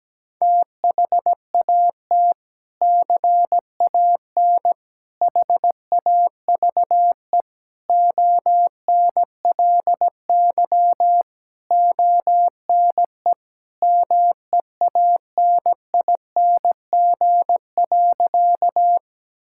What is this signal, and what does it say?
Morse code, heard as a keyed tone, about 17 wpm, 700 Hz. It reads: THAT CAN HAVE ONLY ONE MEANING.
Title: Sentences from Top 1000 Words - Review Part 6 - 17wpm